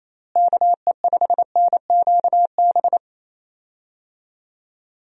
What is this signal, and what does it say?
Morse code, heard as a keyed tone, about 28 wpm, 700 Hz. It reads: KE5NQB
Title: USA Call Signs - 6 Characters - Repeated 2x - 28wpm